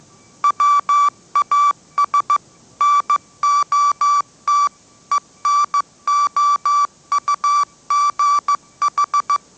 Morse code, heard as a keyed tone, about 15 wpm, 1160 Hz. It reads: WAS NOT ENOUGH
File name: mrx / mrx066 16wpm